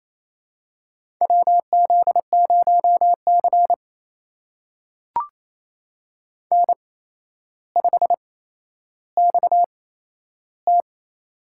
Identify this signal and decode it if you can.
Morse code, heard as a keyed tone, about 28 wpm, spaced as for 12 wpm, 700 Hz. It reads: WZ0C N5XT
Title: USA Call Signs - 4 Characters - 8x Character Spacing - 28wpm